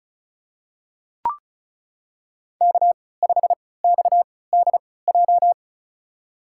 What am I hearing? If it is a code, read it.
Morse code, heard as a keyed tone, about 36 wpm, spaced as for 20 wpm, 700 Hz. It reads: K5XDJ
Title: USA Call Signs - 5 Characters - 3x Character Spacing - 35wpm